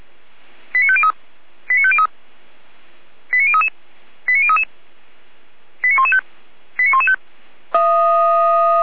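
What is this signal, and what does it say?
Five-tone selective call (ZVEI), heard as quick runs of five tones, tones of about 70 ms. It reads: 89503 89503 89033 89033 89115 89115
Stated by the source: ZVEI five-tone selective call received by SDR.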